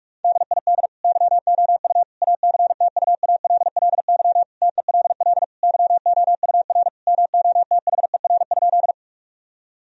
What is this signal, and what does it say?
Morse code, heard as a keyed tone, about 47 wpm, 700 Hz. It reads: DID YOU ACTUALLY TELL YOUR MOTHER?
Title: Sentences from Top 1000 Words - Review Part 4 - 45wpm